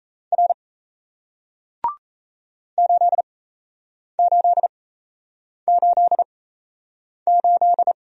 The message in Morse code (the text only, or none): R 8888